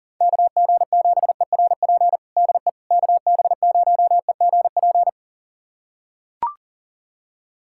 Morse code, heard as a keyed tone, about 40 wpm, 700 Hz. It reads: KG7ERP DE KB0EGP
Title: USA Callsign DE Callsign 40wpm